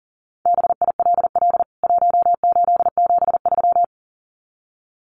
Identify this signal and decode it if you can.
Morse code, heard as a keyed tone, about 40 wpm, 700 Hz. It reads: BILL 1873
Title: CWT Exchange 40wpm